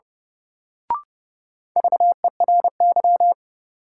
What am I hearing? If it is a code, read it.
Morse code, heard as a keyed tone, about 30 wpm, 700 Hz. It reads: VERY
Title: Sentences from Top 200 Words - New Word Drill - 30wpm